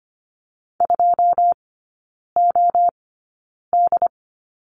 Morse code, heard as a keyed tone, about 25 wpm, spaced as for 9 wpm, 700 Hz. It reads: 2OD
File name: course / 25:10wpm / 093 Lesson 16e - Letter c - Three Characters 25wpm